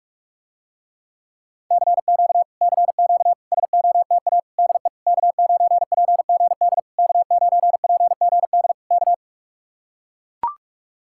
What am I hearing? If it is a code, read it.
Morse code, heard as a keyed tone, about 44 wpm, 700 Hz. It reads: CQ CQ SOTA DE K9PGD K9PGD K